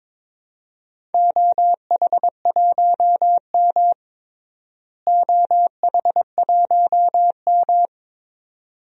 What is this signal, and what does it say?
Morse code, heard as a keyed tone, about 22 wpm, 700 Hz. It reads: OH1M OH1M
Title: DXCC-100 Call Signs - 4 Characters - Repeated 2x - 22wpm